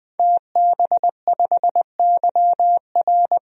Morse code, heard as a keyed tone, about 20 wpm, 700 Hz. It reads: TB5YR
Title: DXCC-100 Call Signs - 5 Characters - 20wpm